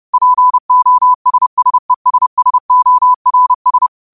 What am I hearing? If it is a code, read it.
Morse code, heard as a keyed tone, about 30 wpm, 1000 Hz. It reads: POSSESSORS
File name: qotdfiles / 30wpm